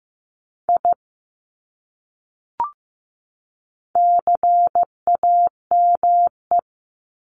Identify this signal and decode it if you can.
Morse code, heard as a keyed tone, about 15 wpm, 700 Hz. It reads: I CAME
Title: Top 300 Words 15wpm